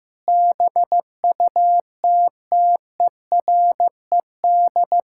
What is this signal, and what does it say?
Morse code, heard as a keyed tone, about 15 wpm, 700 Hz. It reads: BUTTERED
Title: Pangram Challenge v2 - 15wpm